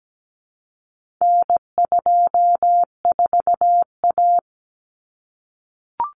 Morse code, heard as a keyed tone, about 17 wpm, 700 Hz. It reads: N24A
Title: Common Cut Numbers - 4-Digit 17wpm